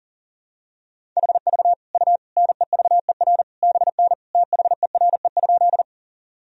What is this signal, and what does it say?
Morse code, heard as a keyed tone, about 40 wpm, 700 Hz. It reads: HV U NEVER BN THERE?
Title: Sentences from Top 300 Words and Abbreviations 40wpm